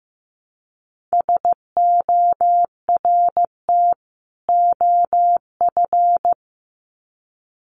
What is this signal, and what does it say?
Morse code, heard as a keyed tone, about 15 wpm, 700 Hz. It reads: SORT OF
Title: Sets of 2 Words 15wpm